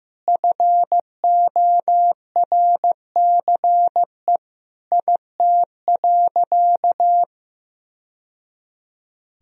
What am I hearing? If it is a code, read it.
Morse code, heard as a keyed tone, about 15 wpm, 700 Hz. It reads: FORCE IT.
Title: Sentences from Top 1000 Words - Review Part 3 - 15wpm